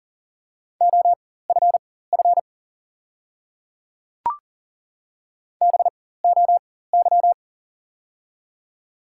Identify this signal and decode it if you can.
Morse code, heard as a keyed tone, about 40 wpm, spaced as for 19 wpm, 700 Hz. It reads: OFF BOY
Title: 4x Character Spacing - Top 200 Words - 40wpm